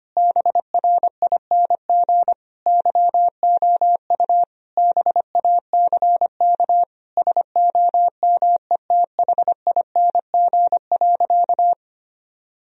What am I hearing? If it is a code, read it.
Morse code, heard as a keyed tone, about 25 wpm, 700 Hz. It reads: BRING YOU BACK SOMETHING.
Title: Sentences from Top 1000 Words - Review Part 3 - 25wpm